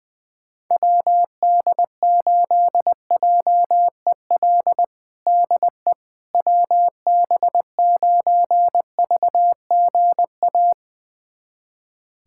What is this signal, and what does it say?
Morse code, heard as a keyed tone, about 20 wpm, 700 Hz. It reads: WD8JEL DE WB9VGA
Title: USA Callsign DE Callsign - 6 Characters - 20wpm